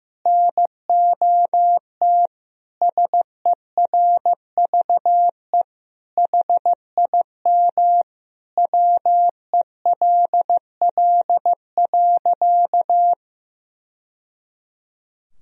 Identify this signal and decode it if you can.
Morse code, heard as a keyed tone, about 15 wpm, 700 Hz. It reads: NOT SERVE HIM WELL.
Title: Sentences from Top 1000 Words - Review Part 2 - 15wpm